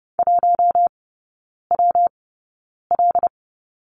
Morse code, heard as a keyed tone, about 30 wpm, 700 Hz. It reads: 1 W L
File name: Lesson 19e - Letter w - Three Characters 30wpm